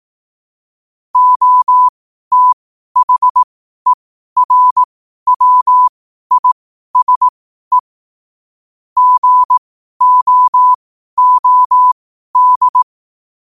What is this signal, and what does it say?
Morse code, heard as a keyed tone, about 18 wpm, spaced as for 12 wpm, 1000 Hz. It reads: OTHERWISE GOOD
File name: 12wpm